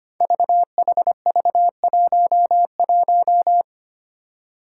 Morse code, heard as a keyed tone, about 25 wpm, 700 Hz. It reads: VHV11
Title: Five Challenging Letter-Number Combinations 25wpm